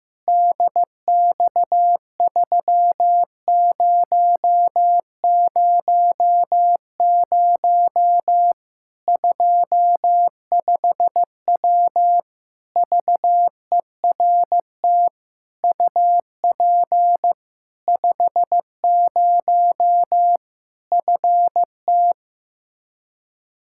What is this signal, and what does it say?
Morse code, heard as a keyed tone, about 16 wpm, 700 Hz. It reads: DX3000 25W VERT UP 50 FT
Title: Practice QSO - Exchange 2 - Info - 15wpm